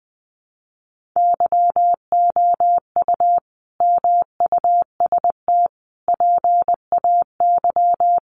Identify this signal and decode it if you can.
Morse code, heard as a keyed tone, about 20 wpm, 700 Hz. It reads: YOU MUST PAY
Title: Sentences from Top 1400 Words 20wpm